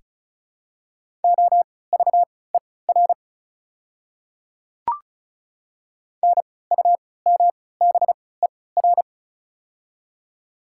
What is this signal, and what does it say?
Morse code, heard as a keyed tone, about 35 wpm, spaced as for 20 wpm, 700 Hz. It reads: OVER NUMBER
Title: 3x Character Spacing - Top 200 Words - 35wpm